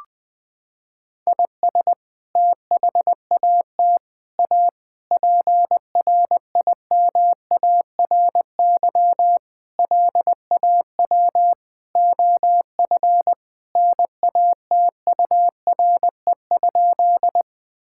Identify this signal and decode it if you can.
Morse code, heard as a keyed tone, about 20 wpm, 700 Hz. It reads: IS THAT A PRIMARY LAW OF NATURE?
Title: Sentences from Top 1000 Words - Review Part 5 - 20wpm